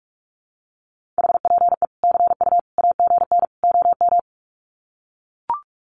Morse code, heard as a keyed tone, about 44 wpm, 700 Hz. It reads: HPE CU AGN OM
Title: Common QSO Phrases 45wpm